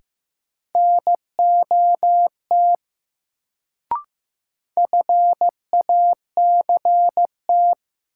Morse code, heard as a keyed tone, about 15 wpm, 700 Hz. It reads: NOT FACT
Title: Sets of 2 Words - New Word Drill - 15wpm